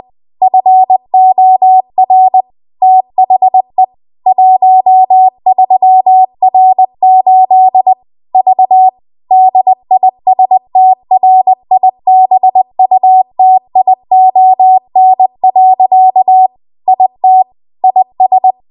Morse code, heard as a keyed tone, about 20 wpm, 750 Hz. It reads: FOR THE 13R8 V DISTRIBUTION. IT IS
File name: Morse / 20WPM